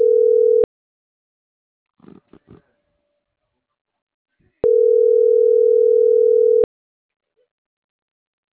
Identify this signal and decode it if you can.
Ringback tone (North America), 440 + 480 Hz.